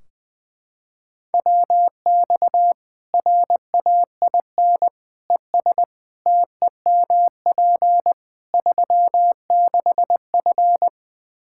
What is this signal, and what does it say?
Morse code, heard as a keyed tone, about 21 wpm, 700 Hz. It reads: WX RAIN ES TEMP 36F
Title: Practice QSO - Exchange 2 - Info - 20wpm